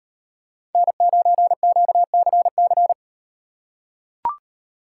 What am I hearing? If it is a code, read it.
Morse code, heard as a keyed tone, about 38 wpm, 700 Hz. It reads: N9QCC